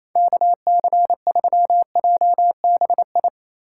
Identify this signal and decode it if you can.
Morse code, heard as a keyed tone, about 28 wpm, 700 Hz. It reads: KC3JBI
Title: USA Call Signs - 3x Character Spacing - 28wpm